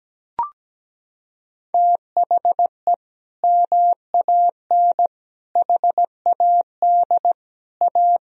Morse code, heard as a keyed tone, about 17 wpm, 700 Hz. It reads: THE MAN HAD A
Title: Sentences from Top 1000 Words - Review Part 2 - 17wpm